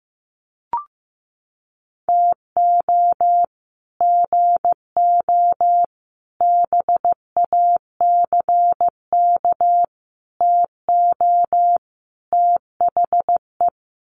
Morse code, heard as a keyed tone, about 15 wpm, 700 Hz. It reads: TO GO BACK TO THE
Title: Sets of 5 Words 15wpm